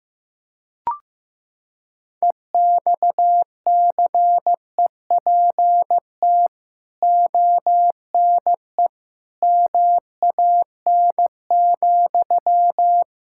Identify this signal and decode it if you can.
Morse code, heard as a keyed tone, about 16 wpm, 700 Hz. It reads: EXCEPT ONE MAN,
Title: Sentences from Top 1600 Words 15wpm